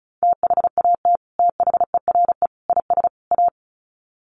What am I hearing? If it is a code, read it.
Morse code, heard as a keyed tone, about 34 wpm, 700 Hz. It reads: THAT THERE IS A
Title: Sets of 4 Words 35wpm